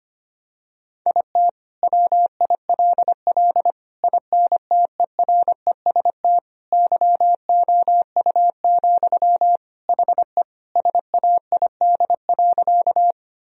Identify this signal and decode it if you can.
Morse code, heard as a keyed tone, about 25 wpm, 700 Hz. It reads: IT WILL INTEREST YOU, HE SAID.
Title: Sentences from Top 1000 Words - Review Part 2 - 25wpm